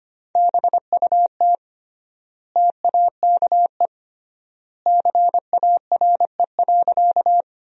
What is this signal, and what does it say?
Morse code, heard as a keyed tone, about 25 wpm, 700 Hz. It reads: BUT TAKE CARE.
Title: 3x Word-Spacing - Sentences from Top 300 Words - 25wpm